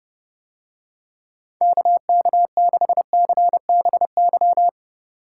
Morse code, heard as a keyed tone, about 29 wpm, 700 Hz. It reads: KK6CBY